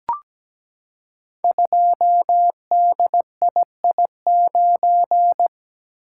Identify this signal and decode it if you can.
Morse code, heard as a keyed tone, about 17 wpm, 700 Hz. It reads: 2DII9
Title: Code Groups - 5 Letter:Numbers - 17wpm